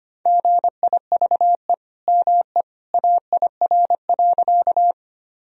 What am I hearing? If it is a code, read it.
Morse code, heard as a keyed tone, about 25 wpm, 700 Hz. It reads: GIVE ME AIR.
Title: Sentences from Top 1000 Words - Review Part 1 - 25wpm